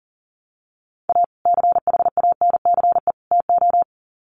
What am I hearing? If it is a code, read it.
Morse code, heard as a keyed tone, about 40 wpm, 700 Hz. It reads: A CHANCE TO
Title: Sets of 3 Words 40wpm